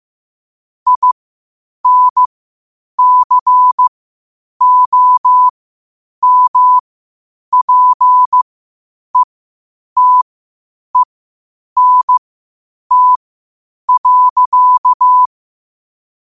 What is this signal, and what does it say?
Morse code, heard as a keyed tone, about 15 wpm, spaced as for 8 wpm, 1000 Hz. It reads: INCOMPETENT.